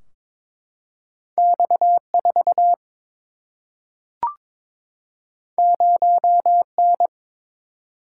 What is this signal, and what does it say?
Morse code, heard as a keyed tone, about 22 wpm, 700 Hz. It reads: X 4 0 N
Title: Single-Letter Single-Number 22wpm